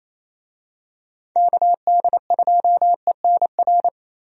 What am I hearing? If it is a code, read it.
Morse code, heard as a keyed tone, about 28 wpm, 700 Hz. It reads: KD2ENR
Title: USA Call Signs - 6 Characters - Repeated 2x - 28wpm